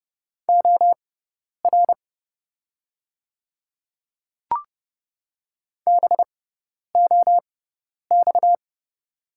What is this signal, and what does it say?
Morse code, heard as a keyed tone, about 30 wpm, spaced as for 10 wpm, 690 Hz. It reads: OR BOX
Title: 6x Character Spacing - Top 500 Words - 30wpm